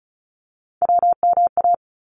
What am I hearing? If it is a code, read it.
Morse code, heard as a keyed tone, about 36 wpm, 700 Hz. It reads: WMA
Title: ARRL Sections 35wpm